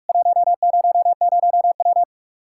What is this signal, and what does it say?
Morse code, heard as a keyed tone, about 41 wpm, 700 Hz. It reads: 100W
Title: Practice QSO - Exchange 2 - Info - 45wpm